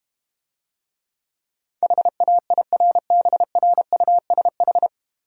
Morse code, heard as a keyed tone, about 32 wpm, 700 Hz. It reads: HAIRBRUSH